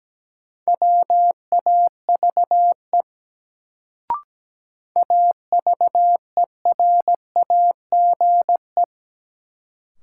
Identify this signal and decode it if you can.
Morse code, heard as a keyed tone, about 17 wpm, 700 Hz. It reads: WAVE AVERAGE
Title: Sentences from Top 1400 Words - New Word Drill Alt - 17wpm